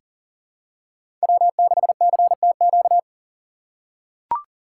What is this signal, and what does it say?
Morse code, heard as a keyed tone, about 40 wpm, 700 Hz. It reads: W6CTQ
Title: USA Call Signs - 5 Characters - 3x Character Spacing - 40wpm